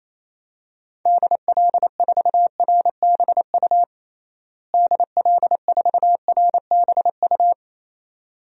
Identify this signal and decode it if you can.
Morse code, heard as a keyed tone, about 28 wpm, 700 Hz. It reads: DL4RBU DL4RBU
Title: DXCC-100 Call Signs - 6 Characters - Repeated 2x - 28wpm